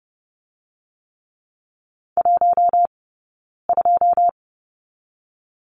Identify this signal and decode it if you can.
Morse code, heard as a keyed tone, about 32 wpm, 700 Hz. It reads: 1 2
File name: Lesson 16d - Letter c - Two Characters 30wpm